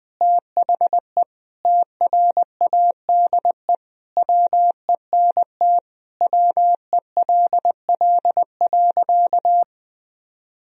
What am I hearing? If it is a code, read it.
Morse code, heard as a keyed tone, about 20 wpm, 700 Hz. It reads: THE TRADE WENT WELL.
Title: Sentences from Top 1000 Words - Review Part 4 - 20wpm